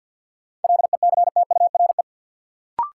Morse code, heard as a keyed tone, about 53 wpm, 700 Hz. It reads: LECTURE
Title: Sentences from Top 1600 Words - New Word Drill - 50wpm